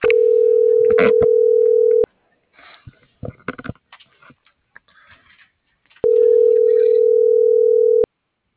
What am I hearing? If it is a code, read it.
Ringback tone (North America), 440 + 480 Hz.
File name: CENTRAL MABE / out edwin.sanchez